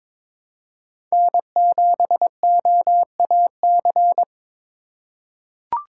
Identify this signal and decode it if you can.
Morse code, heard as a keyed tone, about 22 wpm, 700 Hz. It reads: N7OAC
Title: USA Call Signs - 5 Characters - 4x Character Spacing - 22wpm